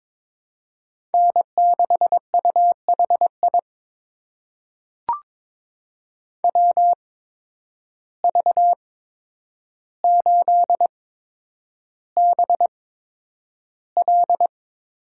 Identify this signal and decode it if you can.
Morse code, heard as a keyed tone, about 22 wpm, spaced as for 9 wpm, 700 Hz. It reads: N6UHI WV8BL
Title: USA Call Signs - 5 Characters - 8x Character Spacing - 22wpm